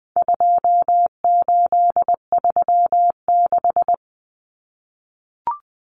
Morse code, heard as a keyed tone, about 20 wpm, 700 Hz. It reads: 2836
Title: Common Cut Numbers - 4-Digit 20wpm